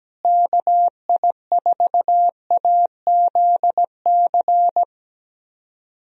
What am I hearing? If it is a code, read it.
Morse code, heard as a keyed tone, about 17 wpm, 700 Hz. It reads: KI4AZC
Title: USA Call Signs - 6 Characters - Repeated 3x - 17wpm